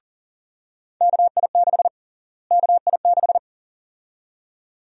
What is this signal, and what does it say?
Morse code, heard as a keyed tone, about 41 wpm, 700 Hz. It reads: KI6 KI6
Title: USA Call Sign Prefixes - Repeated 2x - 40wpm